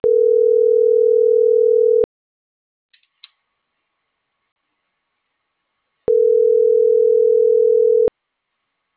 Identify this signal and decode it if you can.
Ringback tone (North America), 440 + 480 Hz.